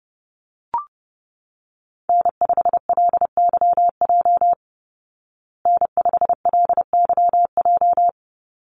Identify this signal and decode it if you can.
Morse code, heard as a keyed tone, about 30 wpm, 700 Hz. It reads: N5LYJ N5LYJ
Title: Call Signs Repeated 2x - 30wpm